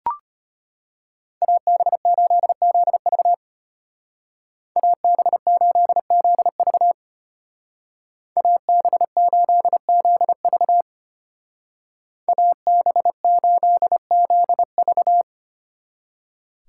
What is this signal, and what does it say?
Morse code, heard as a keyed tone, about 32 wpm, 700 Hz. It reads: AB8ZV AB8ZV AB8ZV AB8ZV